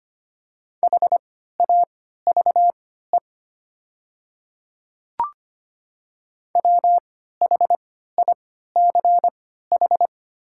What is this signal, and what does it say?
Morse code, heard as a keyed tone, about 25 wpm, spaced as for 14 wpm, 700 Hz. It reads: HAVE WHICH
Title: 3x Character Spacing - Top 100 Words - 25wpm